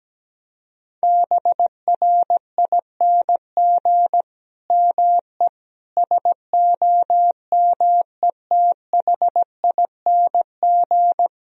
Morse code, heard as a keyed tone, about 17 wpm, 700 Hz. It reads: BRING ME SOMETHING
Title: Sentences from Top 1000 Words - Review Part 3 - 17wpm